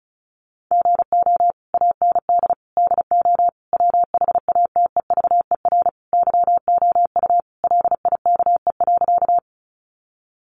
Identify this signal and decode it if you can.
Morse code, heard as a keyed tone, about 34 wpm, 700 Hz. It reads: GO AND DO WHATEVER YOU LIKE.
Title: Sentences from Top 1100 Words 35wpm